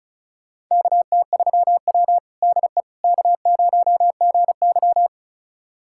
Morse code, heard as a keyed tone, about 36 wpm, 700 Hz. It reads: KT3W DE K0GY
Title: USA Callsign DE Callsign - 4 Characters - 35wpm